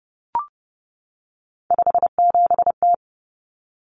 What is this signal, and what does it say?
Morse code, heard as a keyed tone, about 30 wpm, 700 Hz. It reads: 57T
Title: Common Cut Numbers - 3-Digit 30wpm